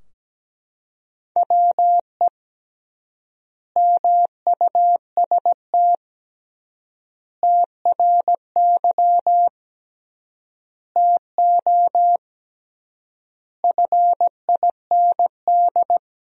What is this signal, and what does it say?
Morse code, heard as a keyed tone, about 17 wpm, 700 Hz. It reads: WE MUST TRY TO FIND
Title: 3x Word-Spacing - Sentences from Top 400 Words - 17wpm